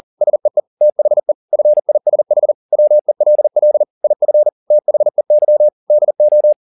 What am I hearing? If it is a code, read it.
Morse code, heard as a keyed tone, about 40 wpm, 600 Hz. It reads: SEE THE FISH WELL IF THEY DO
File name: the old man and the sea 2 40wpm